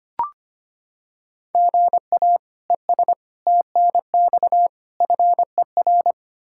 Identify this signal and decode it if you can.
Morse code, heard as a keyed tone, about 25 wpm, 700 Hz. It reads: GA ES TNX FER
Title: K1USN SST QSO Practice 25wpm